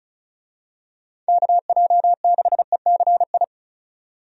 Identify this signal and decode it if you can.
Morse code, heard as a keyed tone, about 36 wpm, 700 Hz. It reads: KJ6ECI
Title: USA Call Signs - 6 Characters - 35wpm